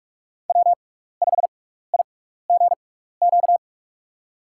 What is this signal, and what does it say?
Morse code, heard as a keyed tone, about 45 wpm, spaced as for 15 wpm, 700 Hz. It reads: W5IGQ
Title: USA Call Signs - 5 Characters - 6x Character Spacing - 45wpm